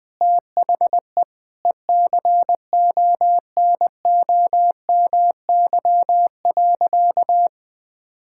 Morse code, heard as a keyed tone, about 20 wpm, 700 Hz. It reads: THE ECONOMY.
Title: Sentences from Top 1000 Words - Review Part 6 - 20wpm